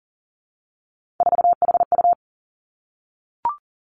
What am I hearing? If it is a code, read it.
Morse code, heard as a keyed tone, about 38 wpm, 700 Hz. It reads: 4HU
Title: Three Challenging Letter-Number Combinations 40wpm